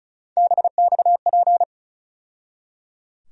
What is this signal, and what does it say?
Morse code, heard as a keyed tone, about 36 wpm, 700 Hz. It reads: BXP
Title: USA Call Sign Postfixes - 3 Letters 35wpm